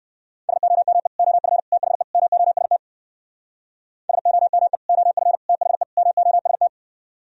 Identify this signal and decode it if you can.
Morse code, heard as a keyed tone, about 58 wpm, 700 Hz. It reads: SOME OF THE MOST SOME OF THE MOST